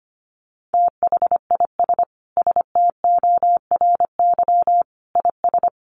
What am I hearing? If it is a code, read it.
Morse code, heard as a keyed tone, about 25 wpm, 700 Hz. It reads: THIS STORY IS
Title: Sentences from Top 1300 Words 25wpm